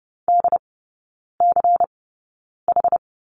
Morse code, heard as a keyed tone, about 30 wpm, spaced as for 9 wpm, 690 Hz. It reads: DCH
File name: Lesson 16e - Letter c - Three Characters 30wpm